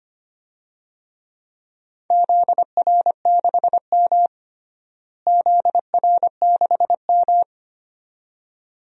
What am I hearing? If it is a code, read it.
Morse code, heard as a keyed tone, about 25 wpm, 700 Hz. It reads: ZR6M ZR6M
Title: DXCC-100 Call Signs - 4 Characters - Repeated 2x - 25wpm